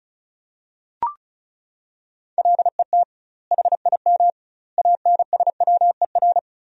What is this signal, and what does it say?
Morse code, heard as a keyed tone, about 34 wpm, 700 Hz. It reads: LET HIM ANSWER
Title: 2x Word-Spacing - Sentences from Top 300 Words - 35wpm